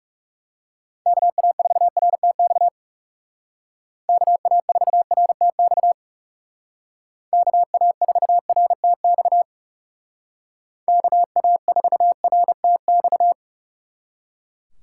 Morse code, heard as a keyed tone, about 37 wpm, 700 Hz. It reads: KA4RTX KA4RTX KA4RTX KA4RTX
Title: Speed Racing - USA 2x3 Call Signs 30wpm